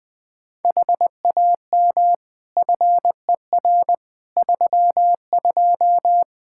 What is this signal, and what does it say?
Morse code, heard as a keyed tone, about 21 wpm, 700 Hz. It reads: HAM FER 32
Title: Practice QSO - Exchange 3 - More Info - 20wpm